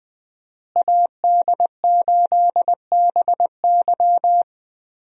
Morse code, heard as a keyed tone, about 21 wpm, 700 Hz. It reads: AD8BY
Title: USA Call Signs - 5 Characters - Repeated 2x - 20wpm